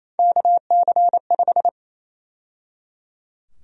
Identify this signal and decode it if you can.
Morse code, heard as a keyed tone, about 27 wpm, 700 Hz. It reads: KC5
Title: USA Call Sign Prefixes - 28wpm